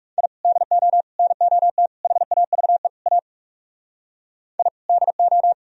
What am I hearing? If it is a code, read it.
Morse code, heard as a keyed tone, about 37 wpm, 700 Hz. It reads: I DO NOT HAVE A I DO
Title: Speed-Racing - Sets of 5 Words 30wpm